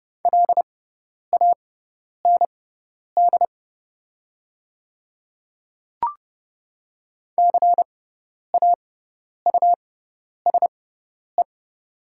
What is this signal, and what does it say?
Morse code, heard as a keyed tone, about 30 wpm, spaced as for 10 wpm, 700 Hz. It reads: LAND CAUSE
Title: 6x Character Spacing - Top 200 Words - 30wpm